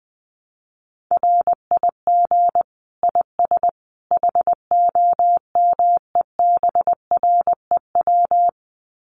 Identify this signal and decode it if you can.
Morse code, heard as a keyed tone, about 19 wpm, 700 Hz. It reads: RIG IS HOMEBREW
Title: Common QSO Phrases 20wpm